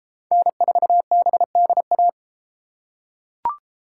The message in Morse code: N4BDA

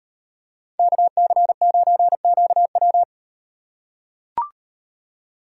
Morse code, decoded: KC9QW

